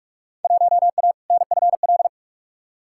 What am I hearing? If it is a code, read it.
Morse code, heard as a keyed tone, about 46 wpm, 700 Hz. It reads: 1A NFL
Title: Field Day Exchanges - 45wpm